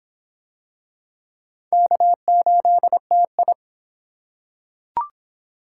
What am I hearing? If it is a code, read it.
Morse code, heard as a keyed tone, about 23 wpm, 700 Hz. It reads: K8TI